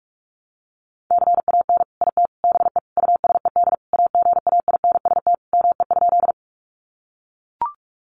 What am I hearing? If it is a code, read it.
Morse code, heard as a keyed tone, about 44 wpm, 700 Hz. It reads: CAN IT BE USED AGAINST ME?
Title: Sentences from Top 600 Words 45wpm